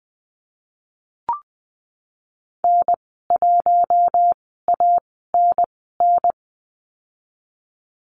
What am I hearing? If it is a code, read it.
Morse code, heard as a keyed tone, about 20 wpm, spaced as for 15 wpm, 700 Hz. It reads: N1ANN
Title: 069 Lesson 12g - Letter d - Call Signs 20wpm